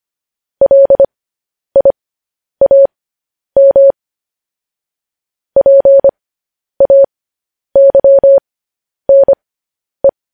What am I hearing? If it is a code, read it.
Morse code, heard as a keyed tone, about 25 wpm, spaced as for 10 wpm, 550 Hz. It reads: LIAM PAYNE
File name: podcast / News headlines-1x-25x10